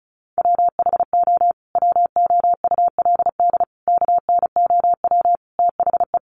Morse code, heard as a keyed tone, about 34 wpm, 700 Hz. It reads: WHO WOULD KNOW THE
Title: Sentences from Top 800 Words 35wpm